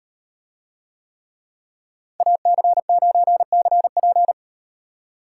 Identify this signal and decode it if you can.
Morse code, heard as a keyed tone, about 38 wpm, 700 Hz. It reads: AC9CP